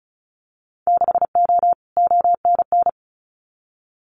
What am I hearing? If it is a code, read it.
Morse code, heard as a keyed tone, about 34 wpm, 700 Hz. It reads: 6O ONN
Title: Winter Field Day Exchanges Encore 35wpm